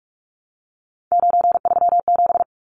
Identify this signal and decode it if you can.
Morse code, heard as a keyed tone, about 44 wpm, 700 Hz. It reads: 937
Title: Numbers - 3-Digit 45wpm